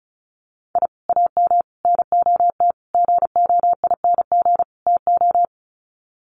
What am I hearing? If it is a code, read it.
Morse code, heard as a keyed tone, about 34 wpm, 700 Hz. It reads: I AM NOT GOING TO
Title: Sets of 5 Words Encore 35wpm